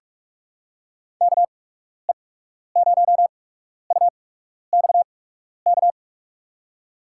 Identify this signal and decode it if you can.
Morse code, heard as a keyed tone, about 44 wpm, spaced as for 12 wpm, 700 Hz. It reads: KE0UXK
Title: USA Call Signs - 8x Character Spacing - 45wpm